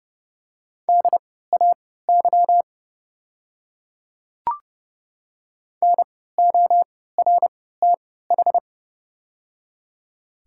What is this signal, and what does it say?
Morse code, heard as a keyed tone, about 31 wpm, spaced as for 17 wpm, 690 Hz. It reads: DAY NORTH
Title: 3x Character Spacing - Top 300 Words - 30wpm